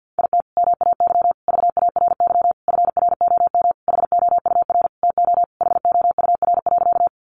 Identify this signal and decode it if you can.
Morse code, heard as a keyed tone, about 50 wpm, 700 Hz. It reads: IT MAY VARY FROM HOUR TO HOUR.
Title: Sentences from Top 1200 Words 50wpm